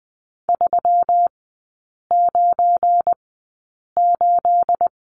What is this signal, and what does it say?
Morse code, heard as a keyed tone, about 20 wpm, spaced as for 8 wpm, 700 Hz. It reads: 398
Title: 212 Lesson 34e - Number 8 - Three Characters 20wpm